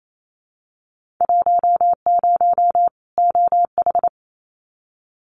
Morse code, heard as a keyed tone, about 28 wpm, 700 Hz. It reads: 10 OH
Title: Winter Field Day Exchanges 28wpm